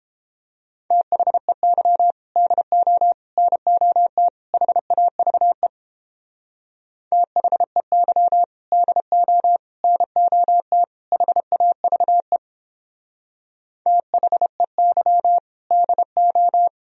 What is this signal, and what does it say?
Morse code, heard as a keyed tone, about 29 wpm, 700 Hz. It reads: THEY DO NOT HAVE THEY DO NOT HAVE THEY DO